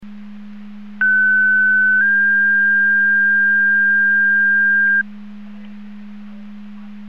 Two-tone page, heard as a one-second tone, then a three-second tone, about 1500 then 1600 Hz. It is heard from 1.0 to 5.0 s.
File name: Portage 154.385 Paging